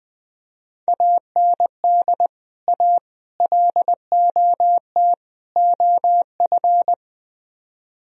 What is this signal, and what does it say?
Morse code, heard as a keyed tone, about 20 wpm, 700 Hz. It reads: AND A LOT OF
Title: Speed-Racing - Sets of 4 Words 17wpm